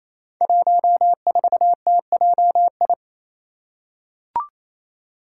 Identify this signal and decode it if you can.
Morse code, heard as a keyed tone, about 28 wpm, 700 Hz. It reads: 14TJI
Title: Five Challenging Letter-Number Combinations 28wpm